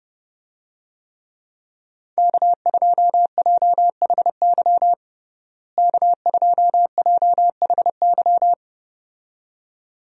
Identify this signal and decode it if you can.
Morse code, heard as a keyed tone, about 30 wpm, 700 Hz. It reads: K2JHY K2JHY
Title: USA Call Signs - 5 Characters - Repeated 2x - 30wpm